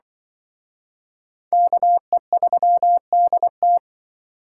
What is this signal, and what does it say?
Morse code, heard as a keyed tone, about 24 wpm, 700 Hz. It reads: KE3DT
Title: Speed Racing - USA 2x2 Call Signs 20wpm